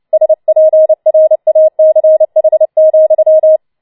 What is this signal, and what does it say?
Morse code, heard as a keyed tone, about 29 wpm, 630 Hz. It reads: SPRACH,
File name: Schneewittchen Teil3 30wpm